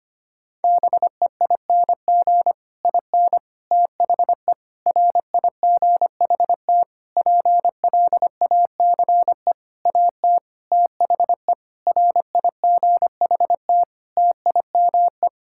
Morse code, heard as a keyed tone, about 25 wpm, 700 Hz. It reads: BEING IN THE RIGHT PLACE AT THE RIGHT TIME
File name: practice / 25wpm / Sentences with 9-Word Phrases - 25wpm